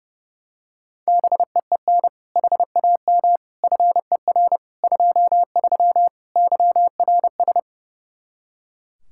Morse code, heard as a keyed tone, about 30 wpm, 700 Hz. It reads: BEEN HAM FER 23 YRS
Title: Practice QSO - Exchange 3 - More Info - 30wpm